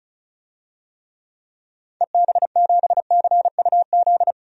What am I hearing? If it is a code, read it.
Morse code, heard as a keyed tone, about 36 wpm, 700 Hz. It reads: EB7CUZ
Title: DXCC-100 Call Signs - 6 Characters - Repeated 2x - 35wpm